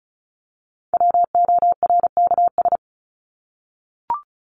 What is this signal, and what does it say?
Morse code, heard as a keyed tone, about 34 wpm, 700 Hz. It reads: WORKS
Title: ES:S Suffix - Top 100 Words with Base Words 35wpm